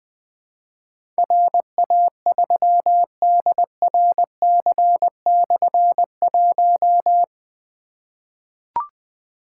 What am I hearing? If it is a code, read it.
Morse code, heard as a keyed tone, about 20 wpm, 700 Hz. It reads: RA3DRC/1
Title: Difficult Call Signs 20wpm